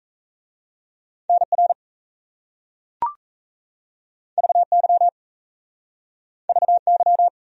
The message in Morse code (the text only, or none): NR VY VY